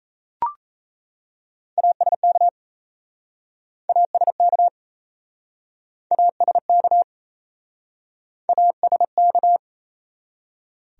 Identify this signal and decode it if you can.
Morse code, heard as a keyed tone, about 37 wpm, 700 Hz. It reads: ASK ASK ASK ASK